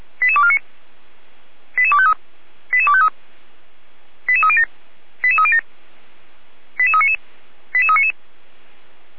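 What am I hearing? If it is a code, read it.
Five-tone selective call (ZVEI), heard as quick runs of five tones, tones of about 70 ms. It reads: 88249 88253 88253 88307 88307 88399 88399
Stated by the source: ZVEI five-tone selective call received on SDR.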